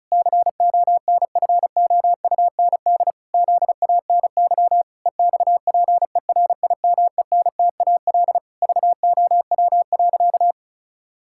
Morse code, heard as a keyed tone, about 34 wpm, 700 Hz. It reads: CONFOUND ZANY EXPERIMENTAL VOW.
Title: Pangram Challenge v2 - 35wpm